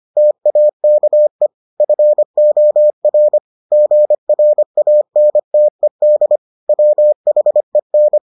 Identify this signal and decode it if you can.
Morse code, heard as a keyed tone, about 25 wpm, 600 Hz. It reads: TAKE FOR GRANTED WHEN